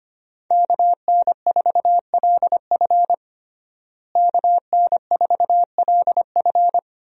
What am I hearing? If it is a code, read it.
Morse code, heard as a keyed tone, about 25 wpm, 700 Hz. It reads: KN4LF KN4LF
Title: USA Call Signs - 5 Characters - Repeated 2x - 25wpm